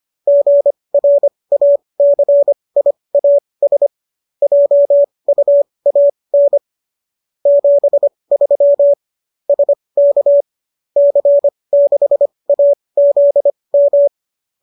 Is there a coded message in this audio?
Yes: Morse code from 0.3 to 14.1 s, heard as a keyed tone, about 25 wpm, spaced as for 20 wpm, 570 Hz. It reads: GRACIAS JUAN 73 SK C6AZM